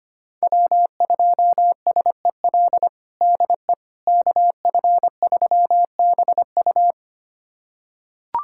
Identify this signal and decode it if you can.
Morse code, heard as a keyed tone, about 25 wpm, 700 Hz. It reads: W2SEL DE KF3BU
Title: USA Callsign DE Callsign - 5 Characters - 25wpm